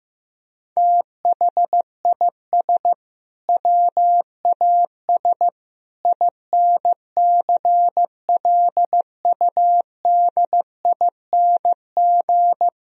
Morse code, heard as a keyed tone, about 15 wpm, 700 Hz. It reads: THIS WAS INCLUDING
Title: Sentences from Top 1000 Words - Review Part 4 - 15wpm